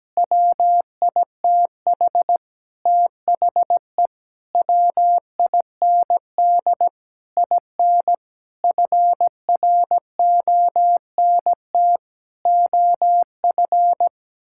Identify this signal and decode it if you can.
Morse code, heard as a keyed tone, about 17 wpm, 700 Hz. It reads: WITH THE WIND IN FRONT OF
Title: Sentences from Top 1000 Words - Review Part 6 - 17wpm